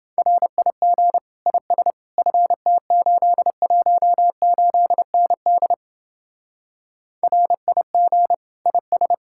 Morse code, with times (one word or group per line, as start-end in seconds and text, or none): 0.2-1.2 RIG
1.5-1.9 IS
2.2-5.7 FT818ND
7.2-8.3 RIG
8.7-9.1 IS